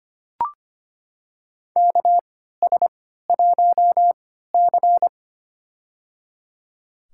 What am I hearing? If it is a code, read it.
Morse code, heard as a keyed tone, about 25 wpm, spaced as for 14 wpm, 700 Hz. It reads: KS1C
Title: USA Call Signs - 3x Character Spacing - 25wpm